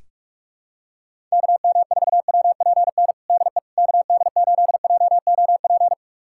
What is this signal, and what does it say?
Morse code, heard as a keyed tone, about 41 wpm, 700 Hz. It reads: KM4WPN DE KD8JOP
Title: USA Callsign DE Callsign - 6 Characters - 45wpm